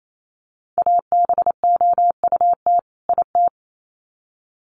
Morse code, heard as a keyed tone, about 28 wpm, 700 Hz. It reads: ABOUT IT
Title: Sets of 2 Words 28wpm